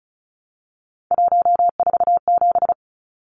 Morse code, heard as a keyed tone, about 34 wpm, 700 Hz. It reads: 147
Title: Numbers - 3-Digit 35wpm